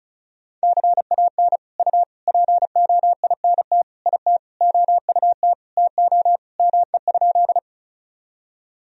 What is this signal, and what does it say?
Morse code, heard as a keyed tone, about 36 wpm, 700 Hz. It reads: CAN U POINT IT OUT TO ME?
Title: Sentences from Top 200 Words and Abbreviations 35wpm